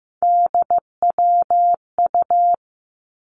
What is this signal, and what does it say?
Morse code, heard as a keyed tone, about 15 wpm, 700 Hz. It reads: DWU
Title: Three Challenging Letter-Number Combinations 15wpm